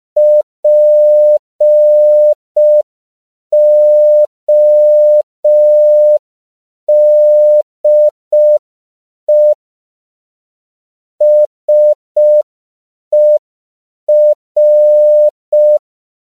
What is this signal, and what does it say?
Morse code, heard as a keyed tone, about 5 wpm, 600 Hz. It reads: PODE SER
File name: CW / 5PPM